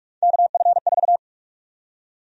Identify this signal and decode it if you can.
Morse code, heard as a keyed tone, about 46 wpm, 700 Hz. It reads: KF4
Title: USA Call Sign Prefixes - 45wpm